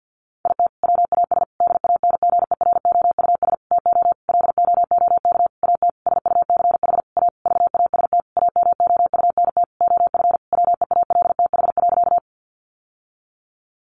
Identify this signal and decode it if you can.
Morse code, heard as a keyed tone, about 50 wpm, 700 Hz. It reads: IT WAS DANGEROUS TO LOOK AT SUCH A VAST AMOUNT OF WEALTH.
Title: Sentences from Top 1300 Words 50wpm